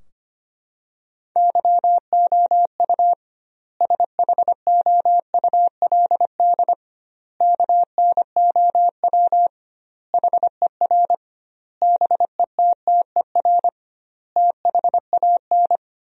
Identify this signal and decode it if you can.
Morse code, heard as a keyed tone, about 25 wpm, 700 Hz. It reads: YOU SHOULD KNOW HER BETTER THAN
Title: 2x Word-Spacing - Sentences from Top 400 Words - 25wpm